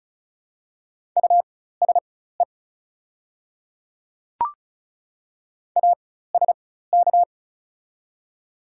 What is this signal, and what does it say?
Morse code, heard as a keyed tone, about 36 wpm, spaced as for 16 wpm, 700 Hz. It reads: USE ASK
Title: 4x Character Spacing - Top 200 Words - 35wpm